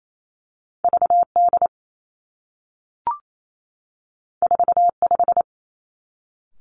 Morse code, heard as a keyed tone, about 28 wpm, 700 Hz. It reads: V D 4 5
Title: Two Challenging Letter-Number Combinations 28wpm